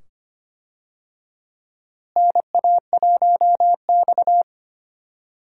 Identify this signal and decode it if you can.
Morse code, heard as a keyed tone, about 25 wpm, 700 Hz. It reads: NA1X
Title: USA Call Signs - 4 Characters - Repeated 2x - 25wpm